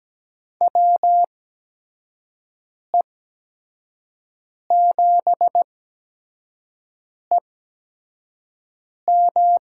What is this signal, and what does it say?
Morse code, heard as a keyed tone, about 17 wpm, spaced as for 5 wpm, 700 Hz. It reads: WE7EM
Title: USA Call Signs - 8x Character Spacing - 17wpm